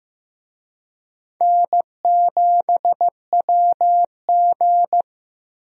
Morse code, heard as a keyed tone, about 15 wpm, 700 Hz. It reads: N7WG
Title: USA Call Signs - 4 Characters - Repeated 2x - 15wpm